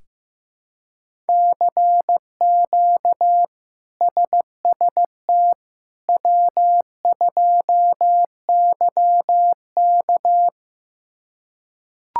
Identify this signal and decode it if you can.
Morse code, heard as a keyed tone, about 15 wpm, 700 Hz. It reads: CQ SST W2YK
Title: K1USN SST QSO Practice - 15wpm